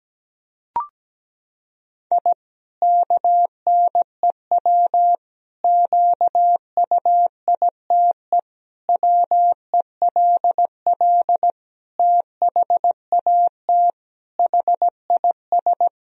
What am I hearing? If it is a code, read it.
Morse code, heard as a keyed tone, about 17 wpm, 700 Hz. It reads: I KNEW QUITE WELL THAT HIS
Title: Sentences from Top 1000 Words - Review Part 3 - 17wpm